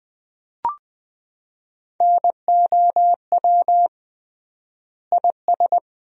Morse code, heard as a keyed tone, about 20 wpm, 700 Hz. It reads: NOW IS
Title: 3x Word-Spacing - Sentences from Top 100 Words - 20wpm